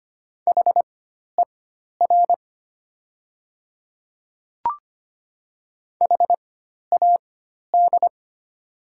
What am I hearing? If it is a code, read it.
Morse code, heard as a keyed tone, about 25 wpm, spaced as for 12 wpm, 700 Hz. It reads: HER HAD
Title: 4x Character Spacing - Top 100 Words - 25wpm